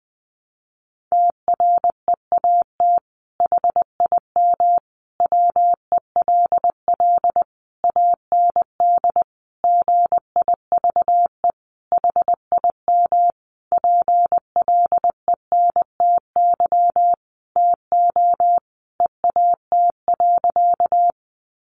Morse code, heard as a keyed tone, about 20 wpm, 700 Hz. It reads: TREAT HIM WELL AND GIVE HIM PLENTY TO EAT.
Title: Sentences from Top 1300 Words 20wpm